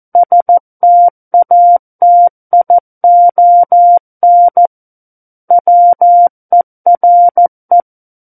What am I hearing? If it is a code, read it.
Morse code, heard as a keyed tone, about 14 wpm, 700 Hz. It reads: STATION WERE
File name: Book One / 05-Chapter 5 - 14 WPM